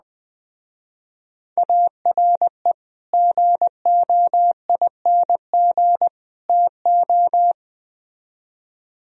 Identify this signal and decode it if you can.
Morse code, heard as a keyed tone, about 21 wpm, 700 Hz. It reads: ARE GOING TO